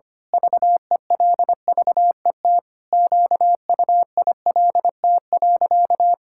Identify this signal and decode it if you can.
Morse code, heard as a keyed tone, about 25 wpm, 700 Hz. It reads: VELVET QUILT.
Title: Pangram Challenge v2 - 25wpm